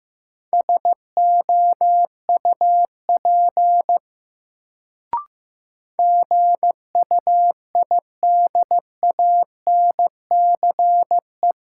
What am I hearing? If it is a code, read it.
Morse code, heard as a keyed tone, about 15 wpm, 700 Hz. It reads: SOUP GUIDANCE
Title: Sentences from Top 1600 Words - New Word Drill Alt - 15wpm